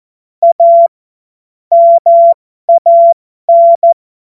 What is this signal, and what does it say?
Morse code, heard as a keyed tone, about 14 wpm, spaced as for 12 wpm, 680 Hz. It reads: A MAN
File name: Morsekurs / Witze